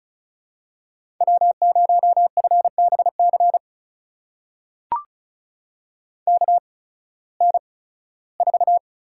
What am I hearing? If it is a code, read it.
Morse code, heard as a keyed tone, about 36 wpm, spaced as for 19 wpm, 700 Hz. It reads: W0FBC KN4
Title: USA Call Signs - 5 Characters - 8x Character Spacing - 35wpm